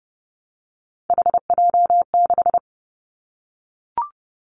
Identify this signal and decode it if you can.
Morse code, heard as a keyed tone, about 30 wpm, 700 Hz. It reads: HJ6
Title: Callsign Prefixes for Top 100 DXCC - 30wpm